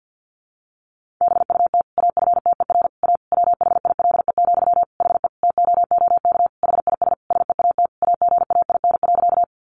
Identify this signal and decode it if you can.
Morse code, heard as a keyed tone, about 50 wpm, 700 Hz. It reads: BUT AFTER A WHILE, HE TOOK HIS SEAT AGAIN.
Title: Sentences from Top 1500 Words 50wpm